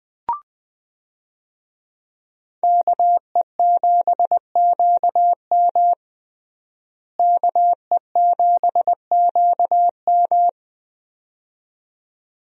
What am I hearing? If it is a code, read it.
Morse code, heard as a keyed tone, about 20 wpm, 700 Hz. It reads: KE7QM KE7QM
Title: USA Call Signs - 5 Characters - Repeated 2x - 20wpm